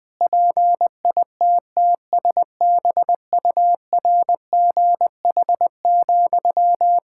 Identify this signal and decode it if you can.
Morse code, heard as a keyed tone, about 20 wpm, 700 Hz. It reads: PITTSBURGH,
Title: Practice QSO - Exchange 1 - Beginning - 20wpm